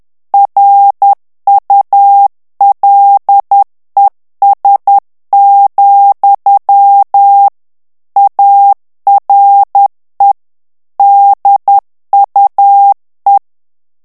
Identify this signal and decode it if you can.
Morse code, heard as a keyed tone, about 10 wpm, 800 Hz. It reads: RULES, ARE DUE
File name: ANGLAIS texte clair 3 vitesse 10 mots minute